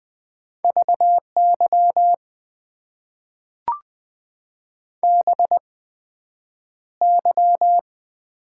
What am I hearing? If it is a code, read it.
Morse code, heard as a keyed tone, about 20 wpm, spaced as for 9 wpm, 700 Hz. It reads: VY BY
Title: Top 90 QSO Elements - 8x Character Spacing - 20wpm